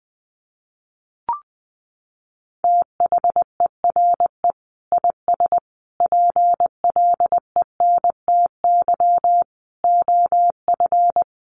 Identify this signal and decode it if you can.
Morse code, heard as a keyed tone, about 20 wpm, 700 Hz. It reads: THERE IS PLENTY OF
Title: Sentences from Top 1300 Words 20wpm